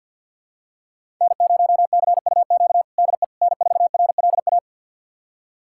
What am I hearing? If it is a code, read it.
Morse code, heard as a keyed tone, about 52 wpm, 700 Hz. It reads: N0CUQ DE N4RLA
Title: USA Callsign DE Callsign - 5 Characters - 50wpm